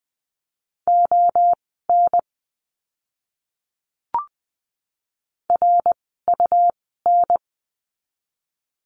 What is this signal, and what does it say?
Morse code, heard as a keyed tone, about 20 wpm, spaced as for 14 wpm, 700 Hz. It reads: ON RUN
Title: 101 Lesson 17f - Letter u - Words 20wpm